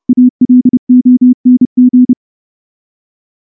Morse code, heard as a keyed tone, about 30 wpm, 250 Hz. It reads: ALONG
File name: morse / morse-training-30WPM